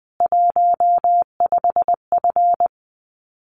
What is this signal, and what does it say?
Morse code, heard as a keyed tone, about 20 wpm, 700 Hz. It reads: 15F
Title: Common QSO Phrases Encore 20wpm